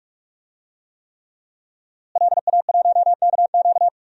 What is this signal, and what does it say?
Morse code, heard as a keyed tone, about 42 wpm, 700 Hz. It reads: LA1KQ
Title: DXCC-100 Call Signs - 5 Characters - Repeated 3x - 45wpm